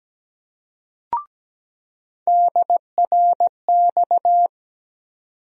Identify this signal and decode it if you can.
Morse code, heard as a keyed tone, about 17 wpm, 700 Hz. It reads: DRX
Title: USA Call Sign Postfixes - 3 Letters 17wpm